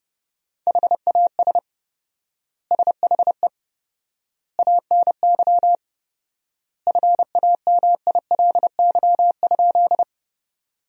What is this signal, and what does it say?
Morse code, heard as a keyed tone, about 30 wpm, 700 Hz. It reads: HAS SHE ANY FAMILY?
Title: 4x Word-Spacing - Sentences from Top 400 Words - 30wpm